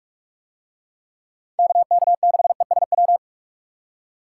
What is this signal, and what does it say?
Morse code, heard as a keyed tone, about 41 wpm, 700 Hz. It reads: KK6ESW